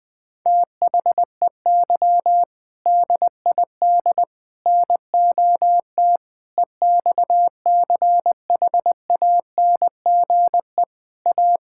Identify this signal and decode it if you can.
Morse code, heard as a keyed tone, about 20 wpm, 700 Hz. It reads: THEY DID NOT EXCHANGE A
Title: Sentences from Top 1000 Words - Review Part 6 - 20wpm